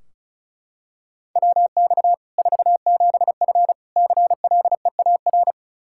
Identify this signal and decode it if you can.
Morse code, heard as a keyed tone, about 34 wpm, 700 Hz. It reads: WX 47F CLEAR
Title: Practice QSO - Exchange 2 - Info - 35wpm